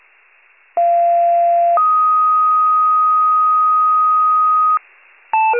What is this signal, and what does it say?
Two-tone page, heard as a one-second tone, then a three-second tone, about 690 then 1250 Hz, with noise. It is heard from 0.8 to 4.8 s.
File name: wcalert